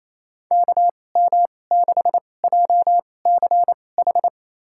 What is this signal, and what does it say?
Morse code, heard as a keyed tone, about 28 wpm, spaced as for 20 wpm, 700 Hz. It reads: KM6JCH
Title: USA Call Signs - 6 Characters - 2x Character Spacing - 28wpm